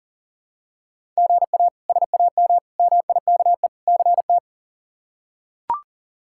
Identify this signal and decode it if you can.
Morse code, heard as a keyed tone, about 40 wpm, 700 Hz. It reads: GA SAM MIKE CT
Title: K1USN SST QSO Practice - 40wpm